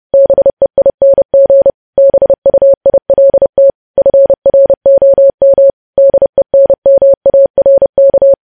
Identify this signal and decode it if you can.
Morse code, heard as a keyed tone, about 30 wpm, 550 Hz. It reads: BEING BUILT FROM DENMARK
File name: podcast / News headlines-1x-30x30